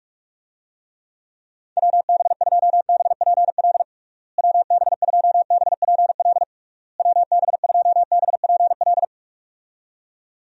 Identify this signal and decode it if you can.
Morse code, heard as a keyed tone, about 42 wpm, 700 Hz. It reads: WB2BPL WB2BPL WB2BPL
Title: USA Call Signs - 6 Characters - Repeated 3x - 45wpm